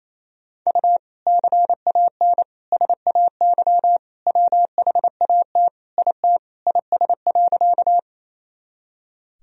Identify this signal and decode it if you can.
Morse code, heard as a keyed tone, about 28 wpm, 700 Hz. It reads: U CAN SAY WHAT IT IS.
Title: Sentences from Top 200 Words and Abbreviations 28wpm